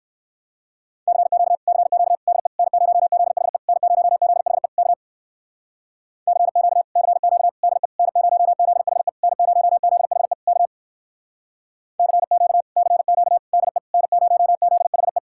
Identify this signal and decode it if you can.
Morse code, heard as a keyed tone, about 59 wpm, 700 Hz. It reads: CQ CQ DE N0ZHE N0ZHE K CQ CQ DE N0ZHE N0ZHE K CQ CQ DE N0ZHE